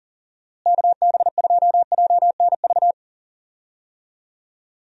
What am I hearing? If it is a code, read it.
Morse code, heard as a keyed tone, about 38 wpm, 700 Hz. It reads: KB2JNV
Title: USA Call Signs - 6 Characters - Repeated 3x - 40wpm